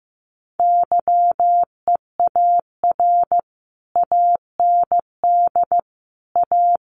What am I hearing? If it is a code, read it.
Morse code, heard as a keyed tone, about 15 wpm, 700 Hz. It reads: YEAR AND A
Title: Sets of 4 Words 15wpm